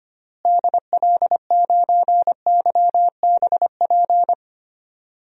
Morse code, heard as a keyed tone, about 25 wpm, 700 Hz. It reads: DL9YBP
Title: DXCC-100 Call Signs - 6 Characters - Repeated 2x - 25wpm